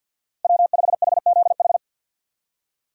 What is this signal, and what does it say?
Morse code, heard as a keyed tone, about 52 wpm, 700 Hz. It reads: W5HZH
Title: USA Call Signs - 5 Characters - Repeated 3x - 50wpm